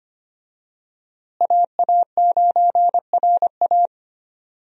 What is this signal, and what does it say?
Morse code, heard as a keyed tone, about 25 wpm, 700 Hz. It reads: AA9RA
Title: USA Call Signs - 5 Characters - Repeated 2x - 25wpm